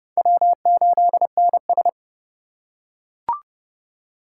Morse code, heard as a keyed tone, about 27 wpm, 700 Hz. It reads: W8NS